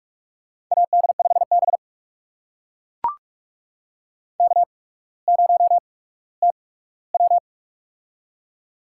Morse code, heard as a keyed tone, about 44 wpm, spaced as for 19 wpm, 690 Hz. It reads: AD5B K0TW
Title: USA Call Signs - 4 Characters - 8x Character Spacing - 45wpm